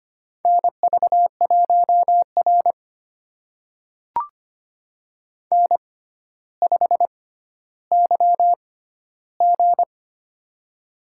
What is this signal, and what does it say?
Morse code, heard as a keyed tone, about 25 wpm, spaced as for 13 wpm, 700 Hz. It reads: NV1R N5YG
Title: USA Call Signs - 4 Characters - 6x Character Spacing - 25wpm